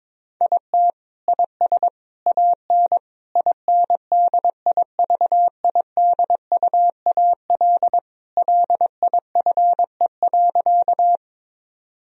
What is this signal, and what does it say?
Morse code, heard as a keyed tone, about 22 wpm, 700 Hz. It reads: IT IS AN INDIVIDUAL LIFE.
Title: Sentences from Top 1000 Words - Review Part 4 - 22wpm